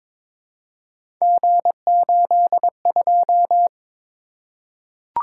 Morse code, heard as a keyed tone, about 22 wpm, 700 Hz. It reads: G82
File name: practice / 22wpm / Three Letter-Number Combinations - 22wpm